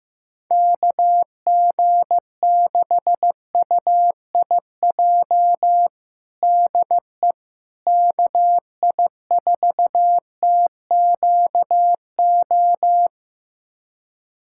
Morse code, heard as a keyed tone, about 15 wpm, 700 Hz. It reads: KG6UIJ DE KI4TQO
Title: USA Callsign DE Callsign - 6 Characters - 15wpm